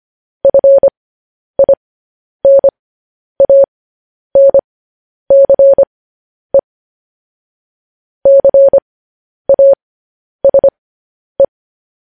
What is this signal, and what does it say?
Morse code, heard as a keyed tone, about 25 wpm, spaced as for 10 wpm, 550 Hz. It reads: FINANCE CASE
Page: news-headlines-in-morse-code-at-10-wpm